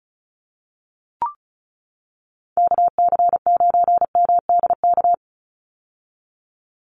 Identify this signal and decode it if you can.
Morse code, heard as a keyed tone, about 34 wpm, 700 Hz. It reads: KC9MDK
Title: USA Call Signs 35wpm